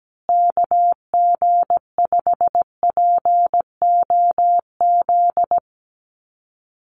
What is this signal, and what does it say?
Morse code, heard as a keyed tone, about 17 wpm, 700 Hz. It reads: KG5POZ
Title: USA Call Signs Repeated 2x 17wpm